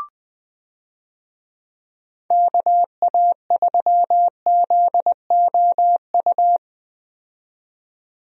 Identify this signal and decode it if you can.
Morse code, heard as a keyed tone, about 20 wpm, 700 Hz. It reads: KA3ZOU
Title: USA Call Signs - 6 Characters - 20wpm